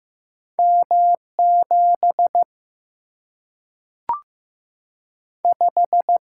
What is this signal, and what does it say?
Morse code, heard as a keyed tone, about 15 wpm, 700 Hz. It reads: M 7 5
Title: Single-Letter Single-Number 15wpm